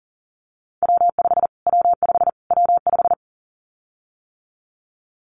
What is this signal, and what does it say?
Morse code, heard as a keyed tone, about 40 wpm, 700 Hz. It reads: W5 W5 W5
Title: Callsign Prefixes for Top 25 DXCC - Repeated 3x - 40wpm